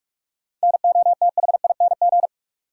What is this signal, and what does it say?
Morse code, heard as a keyed tone, about 46 wpm, 700 Hz. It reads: NOTHING
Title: Sets of 4 Words - New Word Drill - 45wpm